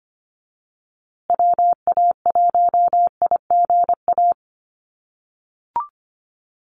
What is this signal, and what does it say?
Morse code, heard as a keyed tone, about 25 wpm, 700 Hz. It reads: WA1IGA
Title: USA Call Signs Repeated 3x 25wpm